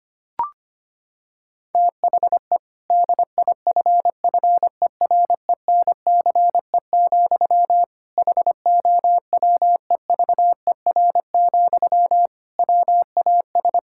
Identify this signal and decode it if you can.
Morse code, heard as a keyed tone, about 25 wpm, 700 Hz. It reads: THE DIFFERENCE, HOWEVER, WAS
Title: Sentences from Top 1000 Words - Review Part 5 - 25wpm